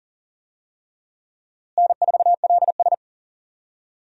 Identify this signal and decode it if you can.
Morse code, heard as a keyed tone, about 37 wpm, 700 Hz. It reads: N4LS
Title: USA Call Signs - 4 Characters - Repeated 3x - 40wpm